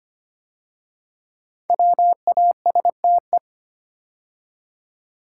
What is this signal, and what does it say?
Morse code, heard as a keyed tone, about 25 wpm, 700 Hz. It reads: WASTE
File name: 5 Letter Words - First 3 Letters Given - 25wpm